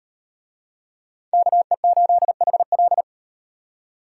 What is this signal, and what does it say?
Morse code, heard as a keyed tone, about 38 wpm, 700 Hz. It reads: KE8HL